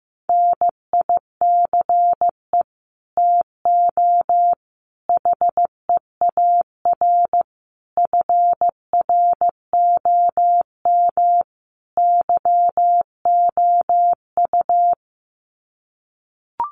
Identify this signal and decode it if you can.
Morse code, heard as a keyed tone, about 15 wpm, 700 Hz. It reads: NICE TO HEAR FROM YOU
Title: Common QSO Phrases 15wpm